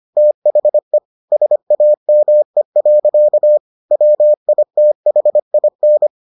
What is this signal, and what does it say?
Morse code, heard as a keyed tone, about 25 wpm, 600 Hz. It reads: THE SAME. WITHIN